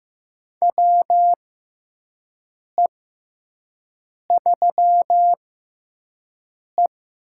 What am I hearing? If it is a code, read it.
Morse code, heard as a keyed tone, about 15 wpm, spaced as for 5 wpm, 700 Hz. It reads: WE3E
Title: USA Call Signs - 4 Characters - 6x Character Spacing - 15wpm